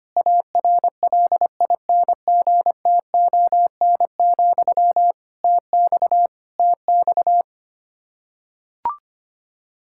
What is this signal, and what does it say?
Morse code, heard as a keyed tone, about 25 wpm, 700 Hz. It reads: ARLINGTON, TX TX
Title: Practice QSO - Exchange 1 - Beginning - 25wpm